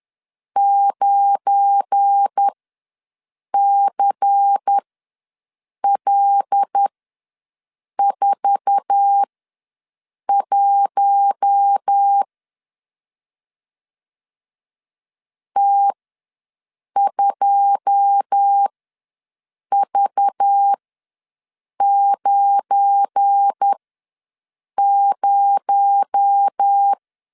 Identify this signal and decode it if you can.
Morse code, heard as a keyed tone, about 10 wpm, spaced as for 6 wpm, 800 Hz. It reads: 9CL41 T2V90
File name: LEZIONI / 10 SIGNS / 10A SLASH ws 10wpm 4121